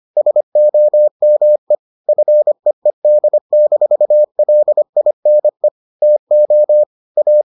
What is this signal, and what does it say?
Morse code, heard as a keyed tone, about 25 wpm, 600 Hz. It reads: SOME FEED-LINE TO A